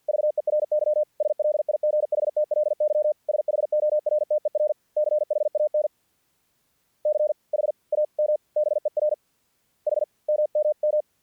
Morse code, heard as a keyed tone, about 50 wpm, spaced as for 44 wpm, 610 Hz. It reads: VERY SLIGHTLY SHORTER CHAN CHAMBER H MMM
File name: archived / CRSnet